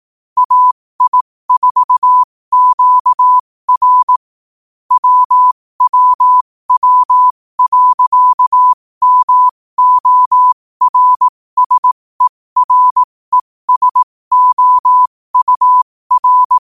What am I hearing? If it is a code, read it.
Morse code, heard as a keyed tone, about 18 wpm, spaced as for 15 wpm, 1000 Hz. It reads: AI4QR WWW.MORSERESOUR
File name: qotdfiles / 15wpm